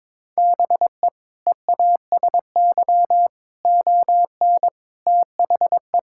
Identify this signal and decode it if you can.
Morse code, heard as a keyed tone, about 22 wpm, 700 Hz. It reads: BE EASY ON THE
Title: Sentences from Top 1600 Words 22wpm